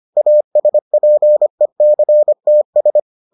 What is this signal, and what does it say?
Morse code, heard as a keyed tone, about 25 wpm, 600 Hz. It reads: ASPECTS